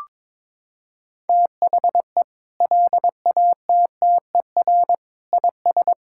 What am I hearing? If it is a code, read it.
Morse code, heard as a keyed tone, about 21 wpm, 700 Hz. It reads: THE LATTER IS
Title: Sentences from Top 1600 Words 22wpm